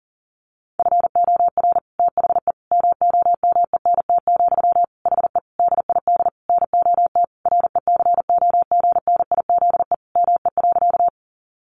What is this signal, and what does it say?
Morse code, heard as a keyed tone, about 40 wpm, 700 Hz. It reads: FOR THE MOMENT, HE DID NOT RECOGNIZE ME.
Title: Sentences from Top 1400 Words 40wpm